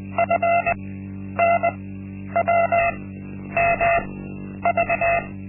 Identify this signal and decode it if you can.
Morse code, heard as a keyed tone, about 20 wpm, spaced as for 10 wpm, 680 Hz. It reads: FNWMV
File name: Linksmakalnis-1